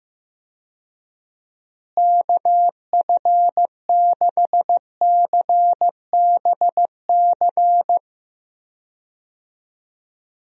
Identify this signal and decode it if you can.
Morse code, heard as a keyed tone, about 16 wpm, 700 Hz. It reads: KF6CBC
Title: USA Call Signs - 6 Characters - Repeated 2x - 15wpm